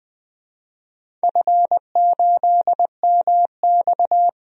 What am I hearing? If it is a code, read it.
Morse code, heard as a keyed tone, about 21 wpm, 700 Hz. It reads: F8MX
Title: DXCC-100 Call Signs - 4 Characters - Repeated 3x - 20wpm